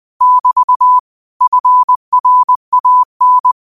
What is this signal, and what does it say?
Morse code, heard as a keyed tone, about 20 wpm, 1000 Hz. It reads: <BT> FRAN
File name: qotdfiles / 20wpm